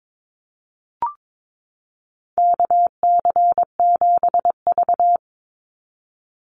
Morse code, heard as a keyed tone, about 22 wpm, 700 Hz. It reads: KC7V
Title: Call Signs - 22wpm